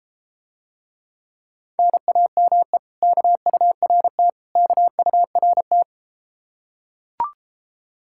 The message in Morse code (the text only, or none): NAME KURT KURT